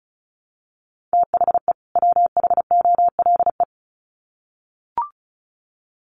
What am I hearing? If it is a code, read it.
Morse code, heard as a keyed tone, about 34 wpm, 700 Hz. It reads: THE WHOLE
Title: Sets of 2 Words 35wpm